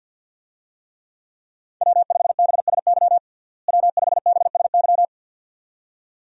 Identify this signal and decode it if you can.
Morse code, heard as a keyed tone, about 52 wpm, 700 Hz. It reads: W5BSY W5BSY
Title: USA Call Signs - 5 Characters - Repeated 2x - 50wpm